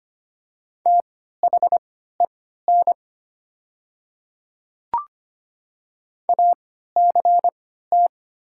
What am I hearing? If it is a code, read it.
Morse code, heard as a keyed tone, about 25 wpm, spaced as for 14 wpm, 700 Hz. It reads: THEN ACT
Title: 3x Character Spacing - Top 300 Words - 25wpm